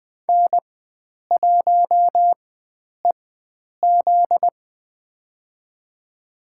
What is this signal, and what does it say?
Morse code, heard as a keyed tone, about 20 wpm, spaced as for 9 wpm, 700 Hz. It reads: N1EZ
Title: USA Call Signs - 4 Characters - 4x Character Spacing - 20wpm